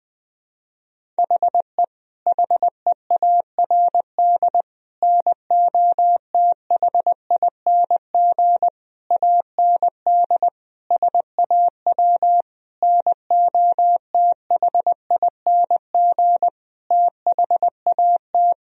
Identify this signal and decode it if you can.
Morse code, heard as a keyed tone, about 20 wpm, 700 Hz. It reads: HE HEARD NOTHING AND SAW NOTHING THAT
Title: Sentences from Top 1000 Words - Review Part 3 - 20wpm